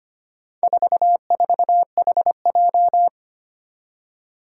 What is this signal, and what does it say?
Morse code, heard as a keyed tone, about 25 wpm, 700 Hz. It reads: 44HJ
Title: Four Challenging Letter-Number Combinations 25wpm